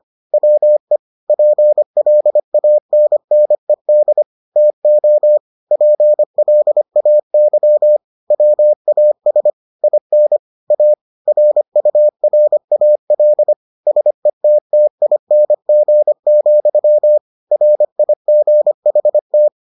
Morse code, heard as a keyed tone, about 25 wpm, 600 Hz. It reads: WE PLANNED TO PLAY WAS IN A RURAL SETTING, RIGHT